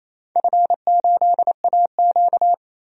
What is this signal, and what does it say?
Morse code, heard as a keyed tone, about 28 wpm, 700 Hz. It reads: F8AQ
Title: DXCC-100 Call Signs - 4 Characters - 28wpm